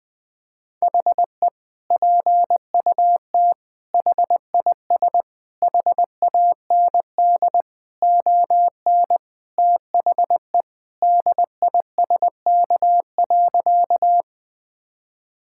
Morse code, heard as a keyed tone, about 20 wpm, 700 Hz. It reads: HE PUT HIS HAND ON THE DISK.
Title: Sentences from Top 1000 Words - Review Part 6 - 20wpm